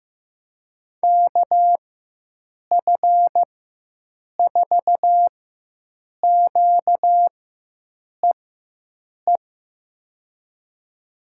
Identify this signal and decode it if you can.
Morse code, heard as a keyed tone, about 15 wpm, spaced as for 7 wpm, 700 Hz. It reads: KF4QEE
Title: USA Call Signs - 6 Characters - 4x Character Spacing - 15wpm